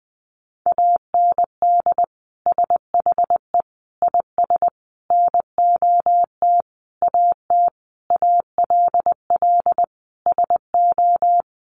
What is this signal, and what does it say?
Morse code, heard as a keyed tone, about 19 wpm, 700 Hz. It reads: AND SHE IS NOT AT ALL SO
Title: Sentences from Top 600 Words 20wpm